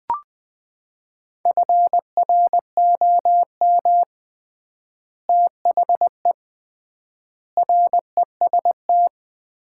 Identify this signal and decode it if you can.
Morse code, heard as a keyed tone, about 20 wpm, 700 Hz. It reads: FROM THE REST
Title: 3x Word-Spacing - Sets of 5 Words - 20wpm